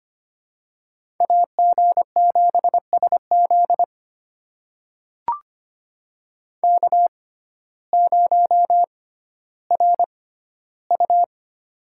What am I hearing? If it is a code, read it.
Morse code, heard as a keyed tone, about 25 wpm, spaced as for 14 wpm, 700 Hz. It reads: AG7SZ K0RU
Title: USA Call Signs - 6x Character Spacing - 25wpm